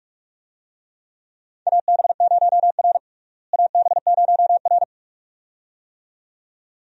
Morse code, heard as a keyed tone, about 43 wpm, 700 Hz. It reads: AB0R AB0R
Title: USA Call Signs - 4 Characters - Repeated 2x - 45wpm